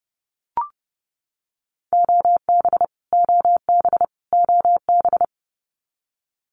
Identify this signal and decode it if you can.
Morse code, heard as a keyed tone, about 30 wpm, 700 Hz. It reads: OB OB OB
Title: Abbreviations Repeated 3x - 30wpm